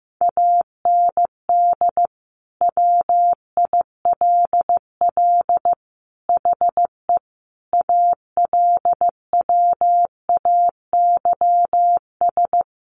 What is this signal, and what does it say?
Morse code, heard as a keyed tone, about 15 wpm, 700 Hz. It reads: AND WILL HE ALWAYS
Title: Sentences from Top 1200 Words 15wpm